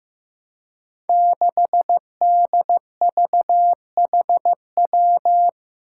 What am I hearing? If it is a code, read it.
Morse code, heard as a keyed tone, about 15 wpm, 700 Hz. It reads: 6DVHW
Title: Five Challenging Letter-Number Combinations 15wpm